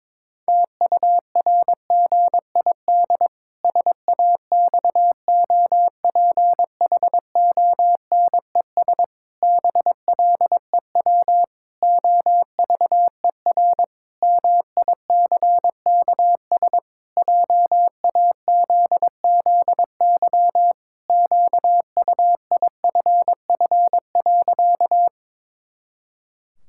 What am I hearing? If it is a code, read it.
Morse code, heard as a keyed tone, about 22 wpm, 700 Hz. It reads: TURGID SAXOPHONES BLEW OVER MICKS JAZZY QUIFF.
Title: Pangram Challenge v2 - 22wpm